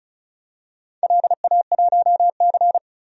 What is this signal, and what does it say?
Morse code, heard as a keyed tone, about 36 wpm, 700 Hz. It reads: LA1C
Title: DXCC-100 Call Signs - 4 Characters - 35wpm